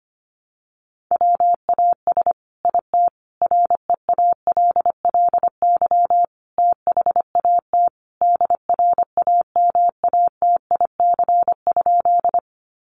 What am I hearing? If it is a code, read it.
Morse code, heard as a keyed tone, about 25 wpm, 700 Hz. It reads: WAS IT REALLY THAT DRAMATIC?
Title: Sentences from Top 1300 Words 25wpm